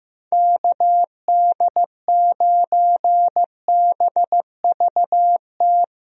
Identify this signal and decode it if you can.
Morse code, heard as a keyed tone, about 15 wpm, 700 Hz. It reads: KD9BVT
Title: USA Call Signs - 6 Characters - Repeated 2x - 15wpm